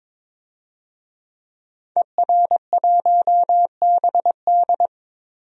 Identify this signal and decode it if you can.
Morse code, heard as a keyed tone, about 22 wpm, 700 Hz. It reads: ER1BD
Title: DXCC-100 Call Signs - 5 Characters - Repeated 3x - 22wpm